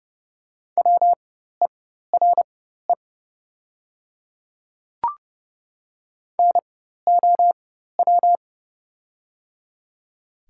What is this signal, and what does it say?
Morse code, heard as a keyed tone, about 30 wpm, spaced as for 14 wpm, 700 Hz. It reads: WERE NOW
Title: 4x Character Spacing - Top 100 Words - 30wpm